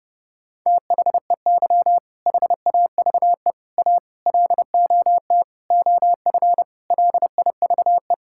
Morse code, heard as a keyed tone, about 30 wpm, 700 Hz. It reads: THEY HAVE A LOT OF LIVE
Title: Sentences from Top 1000 Words - Review Part 5 - 30wpm